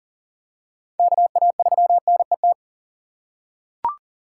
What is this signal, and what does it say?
Morse code, heard as a keyed tone, about 39 wpm, 700 Hz. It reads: KA3NET
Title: USA Call Signs - 6 Characters - 6x Character Spacing - 40wpm